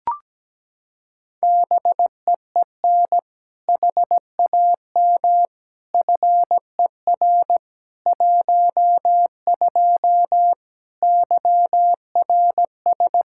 Morse code, heard as a keyed tone, about 17 wpm, 700 Hz. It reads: BEEN HAM FER 12 YRS
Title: Practice QSO - Exchange 3 - More Info - 17wpm